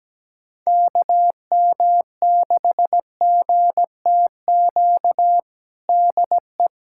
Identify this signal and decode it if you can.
Morse code, heard as a keyed tone, about 17 wpm, 700 Hz. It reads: KM6GTQ DE
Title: USA Callsign DE Callsign - 6 Characters - 17wpm